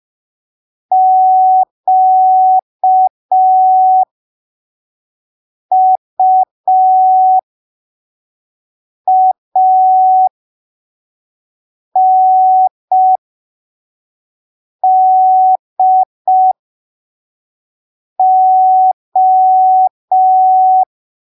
Morse code, heard as a keyed tone, about 5 wpm, spaced as for 3 wpm, 750 Hz. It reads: QUANDO
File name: arquivos / texto / texto 21 05ppm